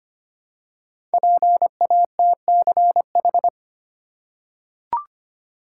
Morse code, heard as a keyed tone, about 25 wpm, 700 Hz. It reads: PATCH
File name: 5 Letter Words - First 3 Letters Given - 25wpm